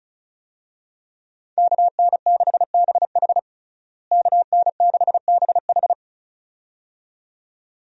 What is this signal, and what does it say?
Morse code, heard as a keyed tone, about 34 wpm, 700 Hz. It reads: KN6BH KN6BH
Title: USA Call Signs - 5 Characters - Repeated 2x - 35wpm